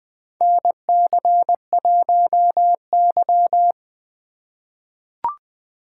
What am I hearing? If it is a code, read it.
Morse code, heard as a keyed tone, about 21 wpm, 700 Hz. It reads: NC1Y
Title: USA Call Signs - 4 Characters - 2x Character Spacing - 20wpm